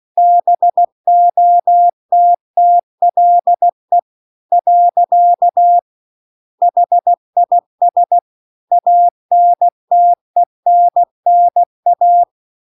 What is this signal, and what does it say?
Morse code, heard as a keyed tone, about 16 wpm, 700 Hz. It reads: BOTTLE . HIS ANTENNA
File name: CW-The-Big-Twoheaded-River-2-16WPM